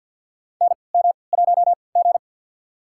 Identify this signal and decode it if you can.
Morse code, heard as a keyed tone, about 46 wpm, spaced as for 27 wpm, 700 Hz. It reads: NM1G
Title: USA Call Signs - 4 Characters - 3x Character Spacing - 50wpm